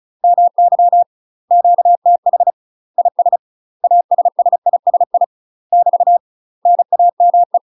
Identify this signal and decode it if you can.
Morse code, heard as a keyed tone, about 36 wpm, 700 Hz. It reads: MY QTH IS ASSISI <BT> NAME
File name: UK MASTER QSOs TRAINING / 35WPM / UK MASTER QSOs 0026